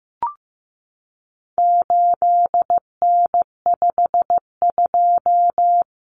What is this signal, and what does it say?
Morse code, heard as a keyed tone, about 15 wpm, 700 Hz. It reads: 8N52
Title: Common Cut Numbers - 4-Digit 15wpm